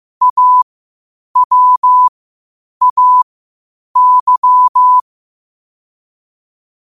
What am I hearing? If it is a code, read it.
Morse code, heard as a keyed tone, about 15 wpm, spaced as for 8 wpm, 1000 Hz. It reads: AWAY